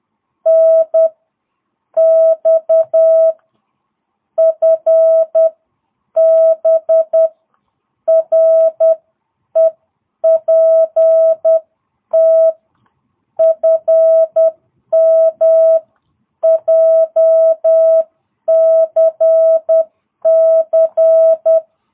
Morse code, heard as a keyed tone, about 10 wpm, spaced as for 8 wpm, 650 Hz. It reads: NX FB REPT FM JCC